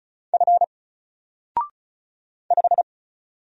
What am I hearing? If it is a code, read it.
Morse code, heard as a keyed tone, about 35 wpm, 700 Hz. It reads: F 5
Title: Single Letter-Number Alt 35wpm